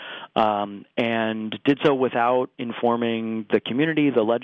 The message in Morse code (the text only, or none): none